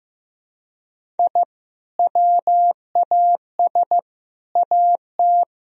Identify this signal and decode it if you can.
Morse code, heard as a keyed tone, about 15 wpm, 700 Hz. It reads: I WAS AT
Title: Sentences from Top 1000 Words - Review Part 1 - 15wpm